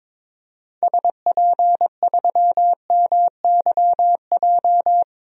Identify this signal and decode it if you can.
Morse code, heard as a keyed tone, about 23 wpm, 700 Hz. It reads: SP3MYJ
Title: DXCC-100 Call Signs - 6 Characters - Repeated 3x - 22wpm